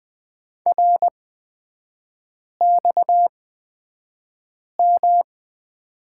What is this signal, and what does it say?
Morse code, heard as a keyed tone, about 20 wpm, spaced as for 5 wpm, 700 Hz. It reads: RXM
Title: 229 Lesson 37e - Letter x - Three Characters 20wpm